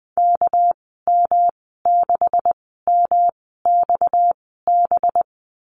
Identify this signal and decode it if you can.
Morse code, heard as a keyed tone, about 20 wpm, spaced as for 14 wpm, 700 Hz. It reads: KM6MXB
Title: 245 Lesson 39g - Letter z - Call Signs 20wpm